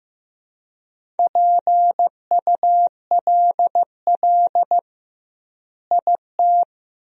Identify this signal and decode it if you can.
Morse code, heard as a keyed tone, about 15 wpm, 700 Hz. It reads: PULL IT
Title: 2x Word-Spacing - Sentences from Top 400 Words - 15wpm